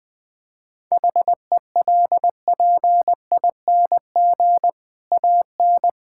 Morse code, heard as a keyed tone, about 20 wpm, 700 Hz. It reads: HELPING AN
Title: Difficult Sentences with Hints - 20wpm